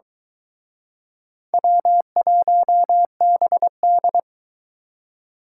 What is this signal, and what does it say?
Morse code, heard as a keyed tone, about 23 wpm, 700 Hz. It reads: W1BD